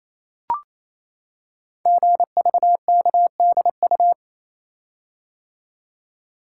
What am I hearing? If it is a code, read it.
Morse code, heard as a keyed tone, about 27 wpm, 700 Hz. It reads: GVKDU
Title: Code Groups - 5 Letter:Numbers - 28wpm